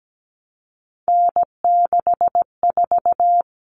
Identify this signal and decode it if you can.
Morse code, heard as a keyed tone, about 17 wpm, 690 Hz. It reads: N64
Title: Common Cut Numbers - 3-Digit 17wpm